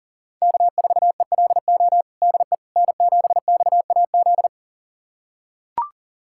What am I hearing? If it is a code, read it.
Morse code, heard as a keyed tone, about 40 wpm, 700 Hz. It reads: K4ELO DE N7XAZ
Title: USA Callsign DE Callsign 40wpm